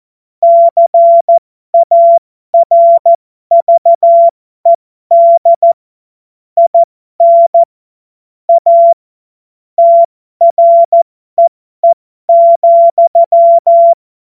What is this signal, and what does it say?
Morse code, heard as a keyed tone, about 14 wpm, spaced as for 12 wpm, 680 Hz. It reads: CARVED IN A TREE,
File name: Morsekurs / Witze